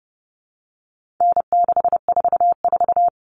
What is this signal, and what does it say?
Morse code, heard as a keyed tone, about 30 wpm, 700 Hz. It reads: N644
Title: All Cut Numbers - 4-Digit 30wpm